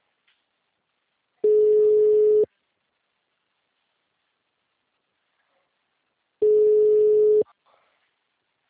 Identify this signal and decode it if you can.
Ringback tone (Europe, 425 Hz).